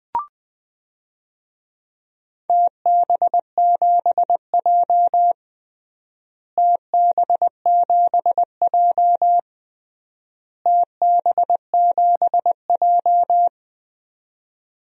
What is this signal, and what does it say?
Morse code, heard as a keyed tone, about 21 wpm, 700 Hz. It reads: TB7J TB7J TB7J
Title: DXCC-100 Call Signs - 4 Characters - Repeated 3x - 20wpm